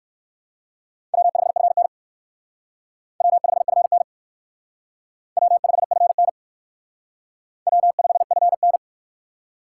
Morse code, heard as a keyed tone, about 56 wpm, 700 Hz. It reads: W5FN W5FN W5FN W5FN
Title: Speed Racing - USA 1x2 Call Signs 45wpm